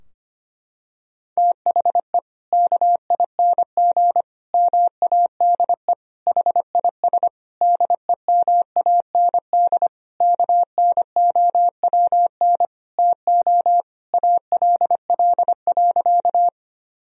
Morse code, heard as a keyed tone, about 25 wpm, 700 Hz. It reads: THE KING MADE HIS DEMAND KNOWN TO ALL.
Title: Sentences from Top 1000 Words - Review Part 4 - 25wpm